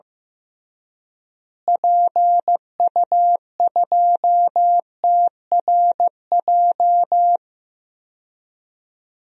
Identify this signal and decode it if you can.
Morse code, heard as a keyed tone, about 15 wpm, 700 Hz. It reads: PU2TRJ
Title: DXCC-100 Call Signs - 6 Characters - Repeated 3x - 15wpm